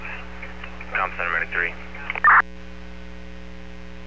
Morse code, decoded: none

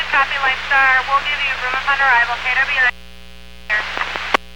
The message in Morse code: none

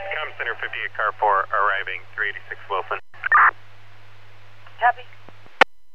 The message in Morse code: none

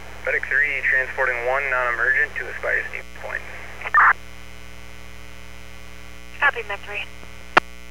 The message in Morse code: none

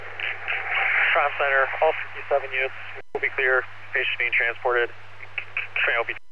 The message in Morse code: none

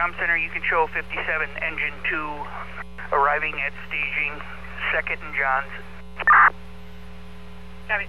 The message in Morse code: none

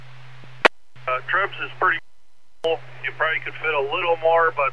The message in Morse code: none

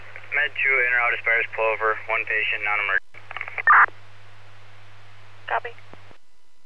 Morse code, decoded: none

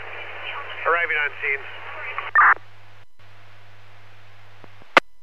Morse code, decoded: none

